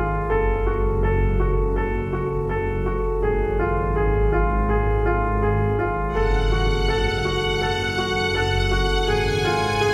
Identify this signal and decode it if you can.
There is no code.